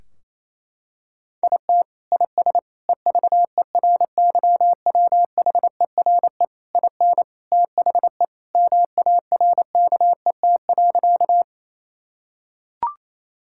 Morse code, heard as a keyed tone, about 28 wpm, 700 Hz. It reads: IT IS EVERYWHERE IN THE MARKET.
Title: Sentences from Top 1600 Words 28wpm